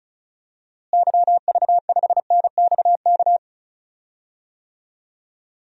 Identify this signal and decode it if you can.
Morse code, heard as a keyed tone, about 36 wpm, 700 Hz. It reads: YV5NXK
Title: DXCC-100 Call Signs - 6 Characters - Repeated 2x - 35wpm